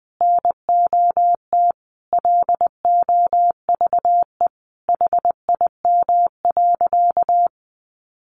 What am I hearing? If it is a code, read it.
Morse code, heard as a keyed tone, about 20 wpm, 700 Hz. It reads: NOT LOVE HIM.
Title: Sentences from Top 600 Words 20wpm